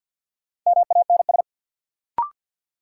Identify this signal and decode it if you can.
Morse code, heard as a keyed tone, about 50 wpm, 700 Hz. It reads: MANS
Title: Sentences from Top 1000 Words - Review Part 6 - New Word Drill - 50wpm